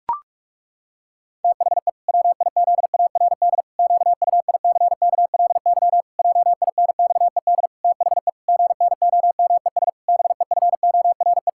Morse code, heard as a keyed tone, about 41 wpm, 700 Hz. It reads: THE WIZARD QUICKLY JINXED THE GNOMES BEFORE
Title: Pangram Challenge v2 - 45wpm